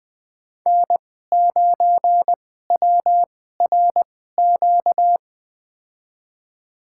Morse code, heard as a keyed tone, about 20 wpm, spaced as for 14 wpm, 700 Hz. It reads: N9WRQ
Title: USA Call Signs - 5 Characters - 2x Character Spacing - 20wpm